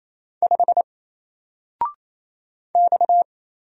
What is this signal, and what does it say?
Morse code, heard as a keyed tone, about 27 wpm, 700 Hz. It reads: 5 X
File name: Single-Letter-Number-28wpm